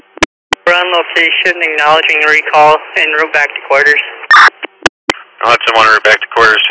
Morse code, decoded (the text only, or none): none